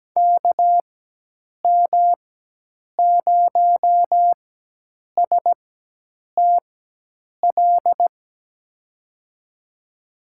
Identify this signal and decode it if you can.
Morse code, heard as a keyed tone, about 17 wpm, spaced as for 8 wpm, 700 Hz. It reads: KM0STL
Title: USA Call Signs - 4x Character Spacing - 17wpm